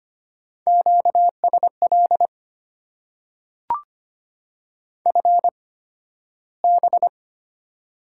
Morse code, heard as a keyed tone, about 25 wpm, spaced as for 13 wpm, 690 Hz. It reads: QSL FB
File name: Top 90 QSO Elements - 8x Character Spacing - 25wpm